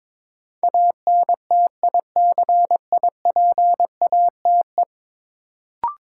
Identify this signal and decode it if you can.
Morse code, heard as a keyed tone, about 21 wpm, 700 Hz. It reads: ANTICIPATE
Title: Sentences from Top 1500 Words - New Word Drill - 22wpm